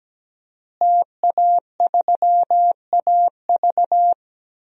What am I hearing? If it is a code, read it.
Morse code, heard as a keyed tone, about 17 wpm, 700 Hz. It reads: TA3AV
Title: DXCC-100 Call Signs - 5 Characters - Repeated 2x - 17wpm